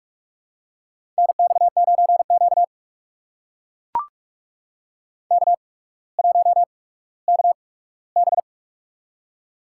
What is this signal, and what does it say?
Morse code, heard as a keyed tone, about 44 wpm, spaced as for 19 wpm, 700 Hz. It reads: NX9Q K1KB
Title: USA Call Signs - 4 Characters - 8x Character Spacing - 45wpm